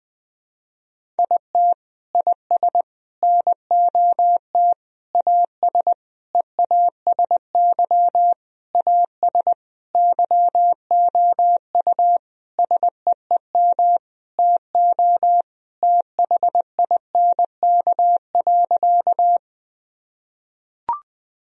Morse code, heard as a keyed tone, about 21 wpm, 700 Hz. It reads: IT IS NOT AS EASY AS YOU SEEM TO THINK.
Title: Sentences from Top 1000 Words - Review Part 1 - 20wpm